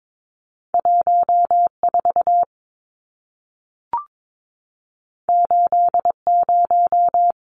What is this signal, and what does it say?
Morse code, heard as a keyed tone, about 22 wpm, 700 Hz. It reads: 14 80
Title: Numbers - 2-Digit 22wpm